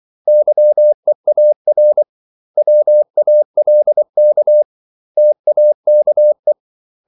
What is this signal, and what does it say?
Morse code, heard as a keyed tone, about 24 wpm, 600 Hz. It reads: YEAR WALK TAKE
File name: CW / Practice / 500MostCommonEnglishWords24